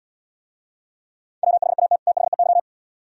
Morse code, heard as a keyed tone, about 77 wpm, 700 Hz. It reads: WHAT THEY